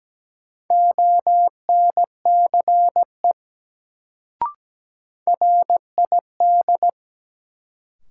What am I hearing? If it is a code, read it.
Morse code, heard as a keyed tone, about 17 wpm, 700 Hz. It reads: ONCE RID
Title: Sets of 4 Words - New Word Drill - 17wpm